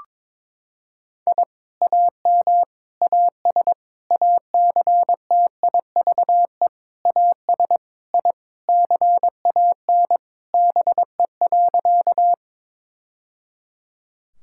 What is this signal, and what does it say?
Morse code, heard as a keyed tone, about 22 wpm, 700 Hz. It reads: I AM AS ACTIVE AS I CAN BE.
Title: Sentences from Top 1000 Words - Review Part 6 - 22wpm